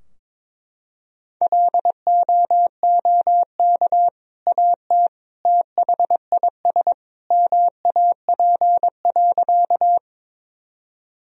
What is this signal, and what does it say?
Morse code, heard as a keyed tone, about 22 wpm, 700 Hz. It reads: LOOK AT THIS MAP.
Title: Sentences from Top 1000 Words - Review Part 2 - 22wpm